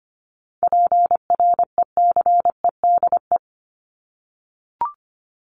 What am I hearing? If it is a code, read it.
Morse code, heard as a keyed tone, about 25 wpm, 700 Hz. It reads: PRECEDE
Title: ING Suffix - Top 100 Words with Base Words 25wpm